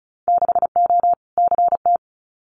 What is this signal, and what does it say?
Morse code, heard as a keyed tone, about 34 wpm, 700 Hz. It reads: 6O CT
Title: Winter Field Day Exchanges Encore 35wpm